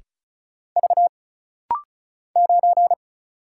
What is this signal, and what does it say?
Morse code, heard as a keyed tone, about 36 wpm, 700 Hz. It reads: V 9
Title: Single Letter-Number Alt - Rapid-Fire 35wpm